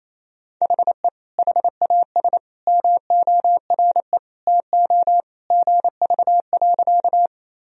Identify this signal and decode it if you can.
Morse code, heard as a keyed tone, about 28 wpm, 700 Hz. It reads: HE HAS MORE TO GV.
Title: Sentences from Top 200 Words and Abbreviations 28wpm